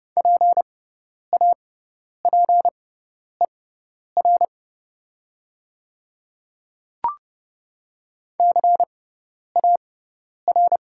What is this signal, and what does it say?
Morse code, heard as a keyed tone, about 30 wpm, spaced as for 10 wpm, 700 Hz. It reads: PAPER CAR
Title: 6x Character Spacing - Top 300 Words - 30wpm